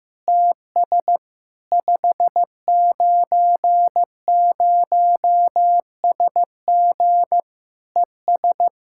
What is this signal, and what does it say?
Morse code, heard as a keyed tone, about 15 wpm, 700 Hz. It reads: TS 590SG ES
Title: Practice QSO - Exchange 2 - Info - 15wpm